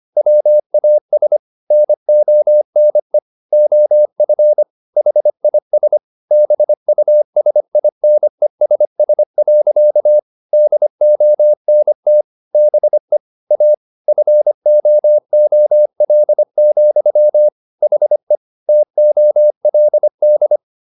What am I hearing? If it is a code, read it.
Morse code, heard as a keyed tone, about 25 wpm, 600 Hz. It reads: WAS NONE OF HIS BUSINESS. DONT BE A FOOL, HE TOLD